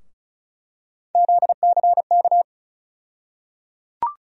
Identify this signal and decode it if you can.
Morse code, heard as a keyed tone, about 36 wpm, 700 Hz. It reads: ZCK
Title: USA Call Sign Postfixes - 3 Letters 35wpm